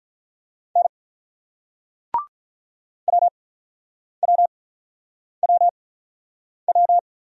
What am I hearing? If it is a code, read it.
Morse code, heard as a keyed tone, about 43 wpm, spaced as for 9 wpm, 700 Hz. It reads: N WWWW
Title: Speed-Racing Single Letters 35wpm